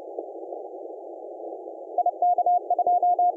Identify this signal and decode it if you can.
Morse code, heard as a keyed tone, about 30 wpm, 710 Hz. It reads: IK2
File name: d66d-comoros-on-15m-cw